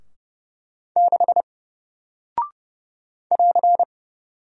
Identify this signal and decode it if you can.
Morse code, heard as a keyed tone, about 30 wpm, 700 Hz. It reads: 6 <AR>
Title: Single Character (Letter:Number:Punctuation:Prosign) Alt 30wpm